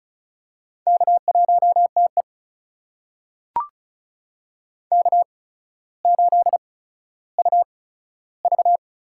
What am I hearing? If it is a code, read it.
Morse code, heard as a keyed tone, about 36 wpm, spaced as for 15 wpm, 700 Hz. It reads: K1TE K8UV
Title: USA Call Signs - 4 Characters - 8x Character Spacing - 35wpm